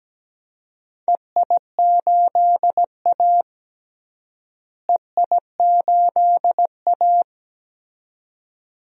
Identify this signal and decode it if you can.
Morse code, heard as a keyed tone, about 17 wpm, 700 Hz. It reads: EI8A EI8A
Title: DXCC-100 Call Signs - 4 Characters - Repeated 2x - 17wpm